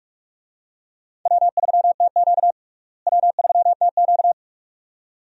Morse code, heard as a keyed tone, about 44 wpm, 700 Hz. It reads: W3TQ W3TQ
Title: USA Call Signs - 4 Characters - Repeated 2x - 45wpm